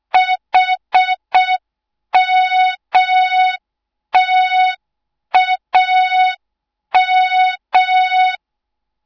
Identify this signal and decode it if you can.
Morse code, heard as a keyed tone, about 6 wpm, 3700 Hz. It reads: HMTAM
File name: Morze-HA2MN / hang / 06BDU HA2MN